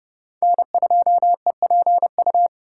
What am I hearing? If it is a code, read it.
Morse code, heard as a keyed tone, about 29 wpm, 700 Hz. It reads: N2EPU